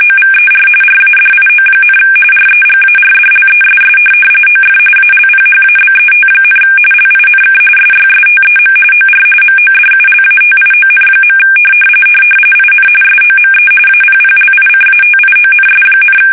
RTTY, about 45.45 baud. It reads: OVER RADIO, REGARDLESS OF ALPHABET, LINK SYSTEM OR MODULATION. IN SOME APPLICATIONS,
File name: fsk850 / teletype-RTTY